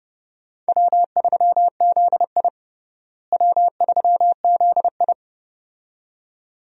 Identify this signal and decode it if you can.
Morse code, heard as a keyed tone, about 31 wpm, 700 Hz. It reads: W3ZI W3ZI
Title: USA Call Signs - 4 Characters - Repeated 2x - 30wpm